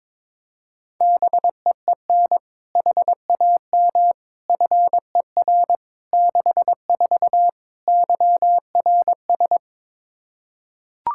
Morse code, heard as a keyed tone, about 22 wpm, 700 Hz. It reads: BEEN HAM FER 64 YRS
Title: K1USN SST QSO Practice 22wpm